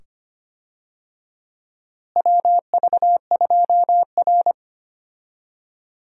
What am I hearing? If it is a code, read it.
Morse code, heard as a keyed tone, about 25 wpm, 700 Hz. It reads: WV2R
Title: USA Call Signs - 4 Characters - Repeated 2x - 25wpm